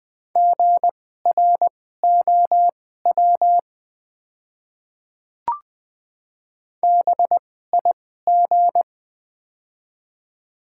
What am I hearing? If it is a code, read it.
Morse code, heard as a keyed tone, about 20 wpm, spaced as for 15 wpm, 700 Hz. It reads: GROW BIG
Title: 2x Character Spacing - Top 300 Words - 20wpm